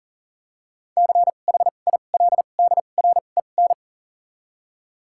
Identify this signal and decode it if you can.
Morse code, heard as a keyed tone, about 40 wpm, 700 Hz. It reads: C H I L D R E N
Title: 2x Character Spacing - Top 500 Words - 40wpm